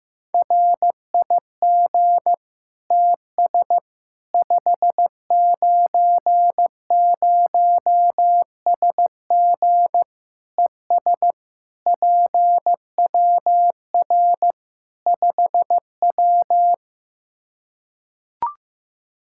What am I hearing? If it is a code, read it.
Morse code, heard as a keyed tone, about 15 wpm, 700 Hz. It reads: RIG TS 590SG ES PWR 5W
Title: Practice QSO - Exchange 2 - Info - 15wpm